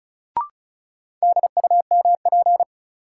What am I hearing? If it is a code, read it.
Morse code, heard as a keyed tone, about 34 wpm, 700 Hz. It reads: DUMP
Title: Sentences from Top 1300 Words - New Word Drill - 35wpm